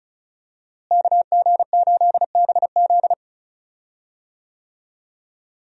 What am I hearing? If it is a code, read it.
Morse code, heard as a keyed tone, about 36 wpm, 700 Hz. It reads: KG8BZ
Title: USA Call Signs - 5 Characters - Repeated 2x - 35wpm